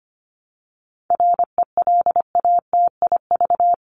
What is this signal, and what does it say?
Morse code, heard as a keyed tone, about 25 wpm, 700 Hz. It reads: RELATIV
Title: LY Suffix - Top 100 Words with Base Words 25wpm